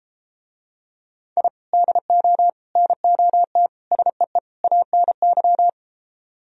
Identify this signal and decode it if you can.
Morse code, heard as a keyed tone, about 33 wpm, 700 Hz. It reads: I DO NOT SEE ANY